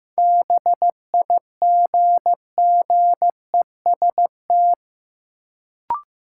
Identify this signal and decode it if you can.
Morse code, heard as a keyed tone, about 15 wpm, 700 Hz. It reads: BIGGEST
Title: Sets of 4 Words - New Word Drill - 15wpm